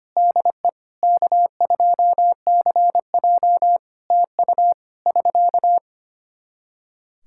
Morse code, heard as a keyed tone, about 25 wpm, 700 Hz. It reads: DE K2CJ TU <SK>
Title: K1USN SST QSO Practice 25wpm